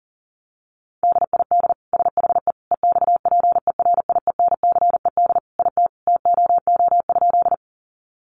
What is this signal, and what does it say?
Morse code, heard as a keyed tone, about 40 wpm, 700 Hz. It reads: DID SHE EXPERIENCED IT TOO?
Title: Sentences from Top 1100 Words 40wpm